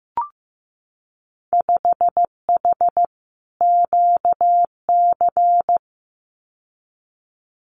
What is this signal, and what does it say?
Morse code, heard as a keyed tone, about 15 wpm, 700 Hz. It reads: 5H QC
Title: Winter Field Day Exchanges Encore 15wpm